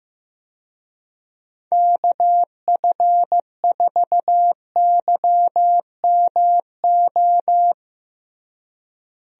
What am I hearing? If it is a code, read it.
Morse code, heard as a keyed tone, about 15 wpm, 700 Hz. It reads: KF4YMO
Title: USA Call Signs - 6 Characters - Repeated 3x - 15wpm